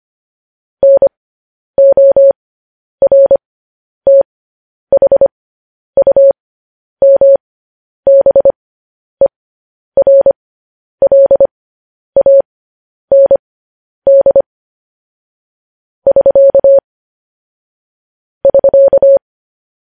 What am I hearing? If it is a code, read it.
Morse code, heard as a keyed tone, about 25 wpm, spaced as for 10 wpm, 550 Hz. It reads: NORTHUMBERLAND <SK> <SK>
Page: news-headlines-in-morse-code-at-10-wpm